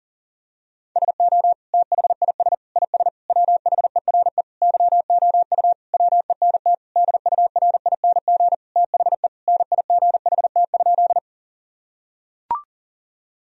Morse code, heard as a keyed tone, about 40 wpm, 700 Hz. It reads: SO THIS IS WHERE YOU WENT DURING THE NIGHT?
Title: Sentences from Top 1000 Words - Review Part 2 - 40wpm